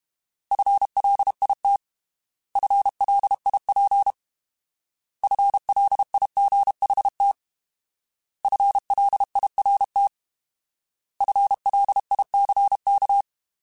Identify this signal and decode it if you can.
Morse code, heard as a keyed tone, about 32 wpm, 800 Hz. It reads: FLIT FLIP FLIGHT FLIRT FLICK